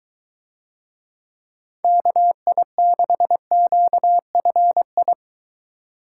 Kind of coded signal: Morse code

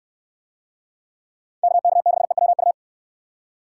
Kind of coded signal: Morse code

